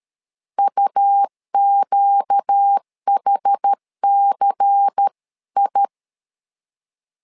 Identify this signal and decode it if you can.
Morse code, heard as a keyed tone, about 13 wpm, 800 Hz. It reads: UQHCI
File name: LEZIONI / 5 LBVC / 5 LBVC 12wpm 1955